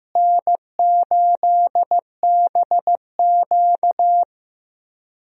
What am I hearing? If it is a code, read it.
Morse code, heard as a keyed tone, about 15 wpm, 700 Hz. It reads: N8BQ
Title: USA Call Signs - 4 Characters - Repeated 3x - 15wpm